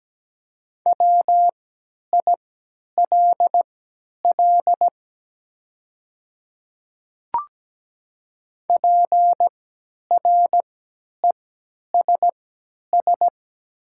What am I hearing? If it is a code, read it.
Morse code, heard as a keyed tone, about 17 wpm, spaced as for 10 wpm, 700 Hz. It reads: WILL PRESS
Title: 3x Character Spacing - Top 500 Words - 17wpm